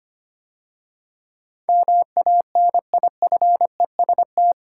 Morse code, heard as a keyed tone, about 25 wpm, 700 Hz. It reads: MANIFEST
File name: 8 Letter Words - First 3 Letters Given - 25wpm